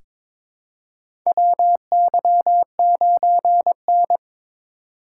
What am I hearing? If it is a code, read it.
Morse code, heard as a keyed tone, about 22 wpm, 700 Hz. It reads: WY9N
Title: USA Call Signs - 4 Characters - 3x Character Spacing - 22wpm